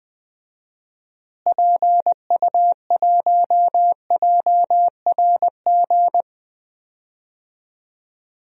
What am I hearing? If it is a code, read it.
Morse code, heard as a keyed tone, about 20 wpm, 700 Hz. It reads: PU1JRG
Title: DXCC-100 Call Signs - 6 Characters - Repeated 2x - 20wpm